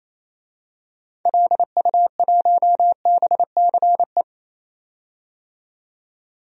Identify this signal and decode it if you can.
Morse code, heard as a keyed tone, about 28 wpm, 700 Hz. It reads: LU1BCE
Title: DXCC-100 Call Signs - 6 Characters - 28wpm